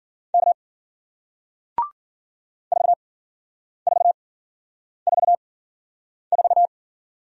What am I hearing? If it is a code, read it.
Morse code, heard as a keyed tone, about 53 wpm, spaced as for 9 wpm, 700 Hz. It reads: K 4444